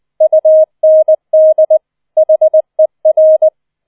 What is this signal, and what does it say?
Morse code, heard as a keyed tone, about 20 wpm, 630 Hz. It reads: UND HER
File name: 20wpm